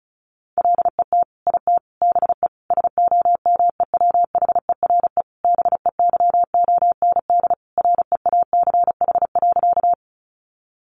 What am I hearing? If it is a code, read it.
Morse code, heard as a keyed tone, about 34 wpm, 700 Hz. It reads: LET IT BE SOMEWHERE BEYOND REACH.
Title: Sentences from Top 1400 Words 35wpm